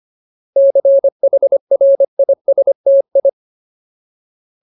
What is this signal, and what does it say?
Morse code, heard as a keyed tone, about 25 wpm, 550 Hz. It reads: CHRISTI
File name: Ultimate Word List / 25WPM Ultimate